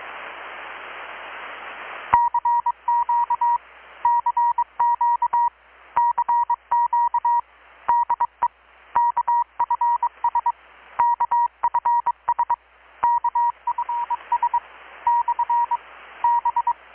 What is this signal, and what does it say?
Morse code, heard as a keyed tone, about 22 wpm, 950 Hz. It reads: CQ CQ CQ DE KFS KFS KFS / B
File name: blog / swl / 6-abril-2019-12695-USB-KPS